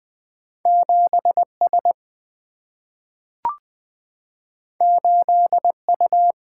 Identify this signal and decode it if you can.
Morse code, heard as a keyed tone, about 20 wpm, 700 Hz. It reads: 7 S 8 U
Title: Single-Letter Single-Number 20wpm